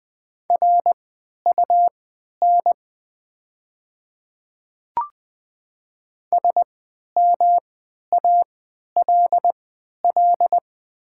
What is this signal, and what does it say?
Morse code, heard as a keyed tone, about 20 wpm, spaced as for 11 wpm, 690 Hz. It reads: RUN SMALL
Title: 3x Character Spacing - Top 300 Words - 20wpm